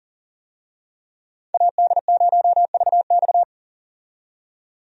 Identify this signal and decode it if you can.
Morse code, heard as a keyed tone, about 40 wpm, 700 Hz. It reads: AD0VX
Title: USA Call Signs - 5 Characters - 40wpm